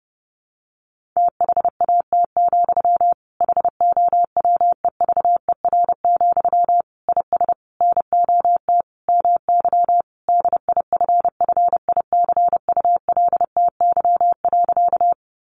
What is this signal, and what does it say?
Morse code, heard as a keyed tone, about 30 wpm, 700 Hz. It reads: THAT, HOWEVER, IS NOT MY DIFFICULTY.
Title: Sentences from Top 1500 Words 30wpm